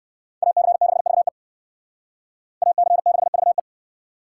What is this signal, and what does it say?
Morse code, heard as a keyed tone, about 56 wpm, 700 Hz. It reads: AC6VE AC6VE
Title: Speed Racing - USA 2x2 Call Signs 45wpm